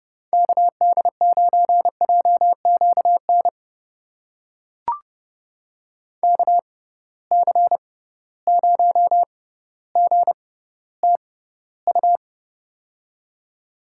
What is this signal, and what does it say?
Morse code, heard as a keyed tone, about 30 wpm, spaced as for 15 wpm, 700 Hz. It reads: KD9JQN KC0GTU
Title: USA Call Signs - 6 Characters - 6x Character Spacing - 30wpm